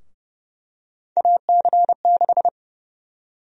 Morse code, heard as a keyed tone, about 30 wpm, 700 Hz. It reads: AC6
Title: USA Call Sign Prefixes - 30wpm